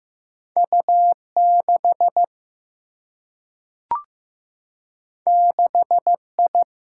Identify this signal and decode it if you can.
Morse code, heard as a keyed tone, about 15 wpm, 700 Hz. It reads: U6 6I
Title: Single-Letter Single-Number 15wpm